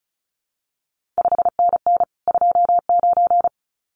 Morse code, heard as a keyed tone, about 36 wpm, 700 Hz. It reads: 5NN 29
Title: CQ WW DX Contest Exchange Encore - 35wpm